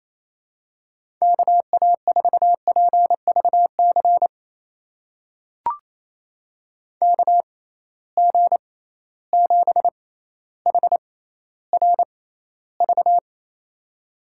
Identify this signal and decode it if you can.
Morse code, heard as a keyed tone, about 28 wpm, spaced as for 14 wpm, 700 Hz. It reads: KA4PVC KG7HRV
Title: USA Call Signs - 6 Characters - 6x Character Spacing - 28wpm